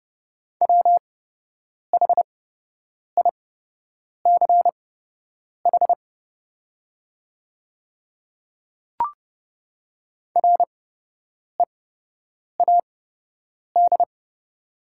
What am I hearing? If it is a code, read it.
Morse code, heard as a keyed tone, about 30 wpm, spaced as for 8 wpm, 690 Hz. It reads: WHICH READ
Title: 8x Character Spacing - Top 200 Words - 30wpm